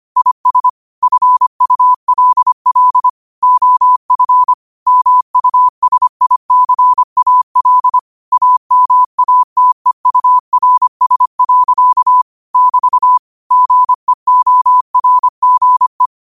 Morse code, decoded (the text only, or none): IS FULL OF MUSICAL AMATEURS. <BT> GEORGE